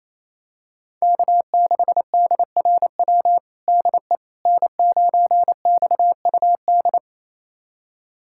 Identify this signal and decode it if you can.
Morse code, heard as a keyed tone, about 28 wpm, 700 Hz. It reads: K6DRW DE N9XUD
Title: USA Callsign DE Callsign - 5 Characters - 28wpm